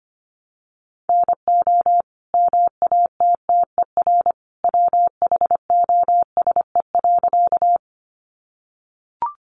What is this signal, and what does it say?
Morse code, heard as a keyed tone, about 25 wpm, 700 Hz. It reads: NO MATTER WHOSE.
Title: Sentences from Top 1200 Words 25wpm